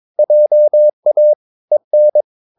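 Morse code, heard as a keyed tone, about 22 wpm, 600 Hz. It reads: JA EN
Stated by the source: Morse practice of short Danish words at 22 wpm, from CW med Gnister.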